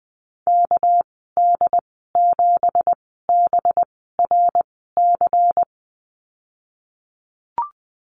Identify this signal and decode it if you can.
Morse code, heard as a keyed tone, about 20 wpm, spaced as for 14 wpm, 700 Hz. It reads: KD7BRC
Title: 200 Lesson 32g - Letter k - Call Signs 20wpm